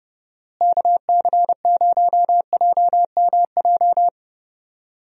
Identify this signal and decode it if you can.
Morse code, heard as a keyed tone, about 30 wpm, 700 Hz. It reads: KC0JMJ
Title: USA Call Signs - 6 Characters - Repeated 3x - 30wpm